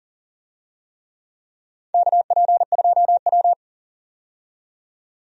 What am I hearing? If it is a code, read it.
Morse code, heard as a keyed tone, about 40 wpm, 700 Hz. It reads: KP2W
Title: USA Call Signs - 4 Characters - 40wpm